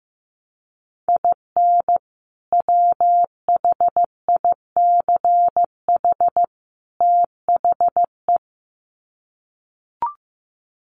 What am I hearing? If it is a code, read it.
Morse code, heard as a keyed tone, about 15 wpm, 700 Hz. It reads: IN WHICH THE
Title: Sets of 3 Words 15wpm